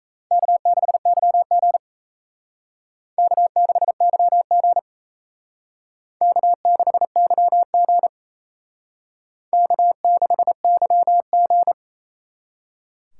Morse code, heard as a keyed tone, about 35 wpm, 700 Hz. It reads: K6YG K6YG K6YG K6YG
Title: Speed Racing - USA 1x2 Call Signs 28wpm